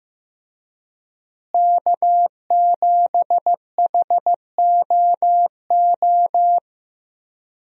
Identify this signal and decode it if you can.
Morse code, heard as a keyed tone, about 15 wpm, 700 Hz. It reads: K7HOO
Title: USA Call Signs - 5 Characters - Repeated 3x - 15wpm